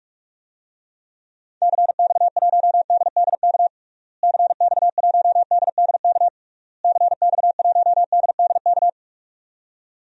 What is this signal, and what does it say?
Morse code, heard as a keyed tone, about 45 wpm, 700 Hz. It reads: CX1DDK CX1DDK CX1DDK
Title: DXCC-100 Call Signs - 6 Characters - Repeated 3x - 45wpm